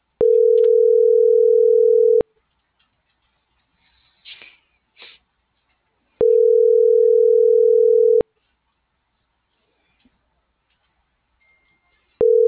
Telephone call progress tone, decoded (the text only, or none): ringback tone